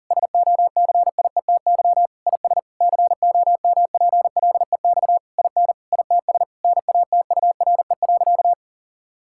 Morse code, heard as a keyed tone, about 40 wpm, 700 Hz. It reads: SOCIETY IS COMPLEX IN ITS NATURE.
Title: Sentences from Top 1000 Words - Review Part 6 - 40wpm